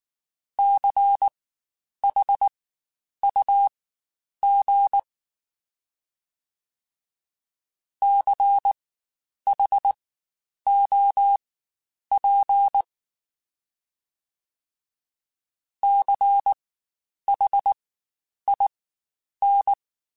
Morse code, heard as a keyed tone, about 19 wpm, spaced as for 9 wpm, 800 Hz. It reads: CHUG CHOP CHIN
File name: Spaced Alphas / Spaced Alphas mp3s / Cs sp @18